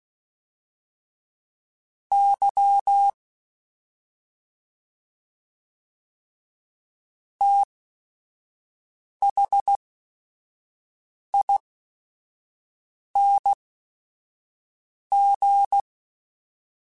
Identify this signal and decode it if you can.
Morse code, heard as a keyed tone, about 16 wpm, spaced as for 5 wpm, 800 Hz. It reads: Y THING
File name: Special INGs / Special INGs-mp3s / SlowToFaster ing @5